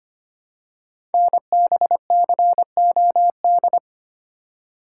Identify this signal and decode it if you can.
Morse code, heard as a keyed tone, about 25 wpm, 700 Hz. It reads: NBCOD
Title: Code Groups - 5 Letters - 25wpm